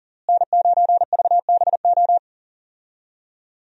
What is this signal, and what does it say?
Morse code, heard as a keyed tone, about 40 wpm, 700 Hz. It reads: N9VBO
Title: USA Call Signs - 5 Characters - Repeated 3x - 40wpm